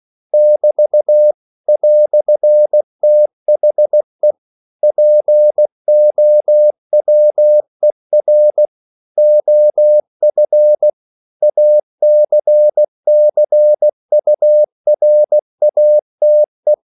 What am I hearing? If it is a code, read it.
Morse code, heard as a keyed tone, about 16 wpm, 600 Hz. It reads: <BT> "THE POWER OF ACCURATE
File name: CW / Practice / EnglishQuotations0116